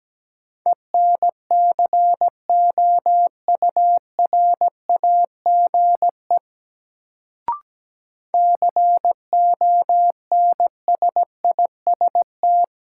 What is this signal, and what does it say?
Morse code, heard as a keyed tone, about 17 wpm, 700 Hz. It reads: ENCOURAGE CONSIST
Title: Sentences from Top 1400 Words - New Word Drill - 17wpm